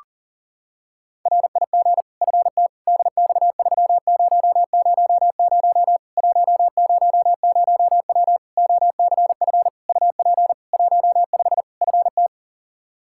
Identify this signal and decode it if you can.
Morse code, heard as a keyed tone, about 40 wpm, 700 Hz. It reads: RIG FT DX3000 100W OCF UP 15 FT
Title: K1USN SST QSO Practice 40wpm